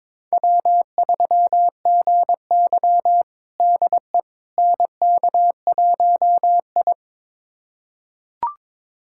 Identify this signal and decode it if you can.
Morse code, heard as a keyed tone, about 22 wpm, 700 Hz. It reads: W3GY DE NK1I
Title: USA Callsign DE Callsign - 4 Characters - 22wpm